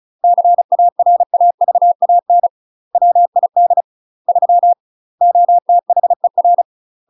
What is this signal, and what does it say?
Morse code, heard as a keyed tone, about 36 wpm, 700 Hz. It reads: CARAVAN WID 3 OTHER